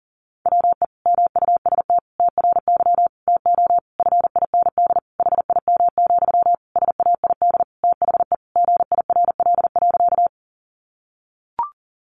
Morse code, heard as a keyed tone, about 40 wpm, 700 Hz. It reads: WE MUST TRY TO FIND HIM, SAID THE GIRL.
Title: Sentences from Top 400 Words 40wpm